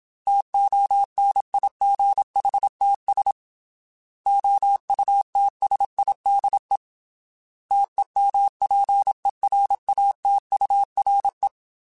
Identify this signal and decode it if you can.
Morse code, heard as a keyed tone, about 27 wpm, 800 Hz. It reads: TONIGHTS OUTSIDE TEMPERATURE
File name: Weather Sets / Weather Sets-mp3s / Weather 5 12 @25